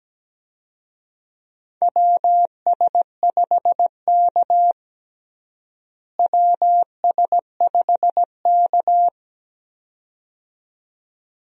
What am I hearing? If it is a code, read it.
Morse code, heard as a keyed tone, about 17 wpm, 700 Hz. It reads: WS5K WS5K
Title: USA Call Signs - 4 Characters - Repeated 2x - 17wpm